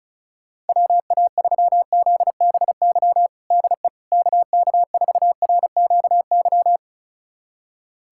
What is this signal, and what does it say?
Morse code, heard as a keyed tone, about 34 wpm, 700 Hz. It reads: WA3ZBY DE KK4RQY
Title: USA Callsign DE Callsign 35wpm